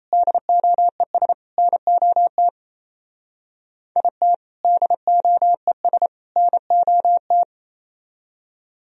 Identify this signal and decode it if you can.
Morse code, heard as a keyed tone, about 36 wpm, 700 Hz. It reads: DOES NOT IT DOES NOT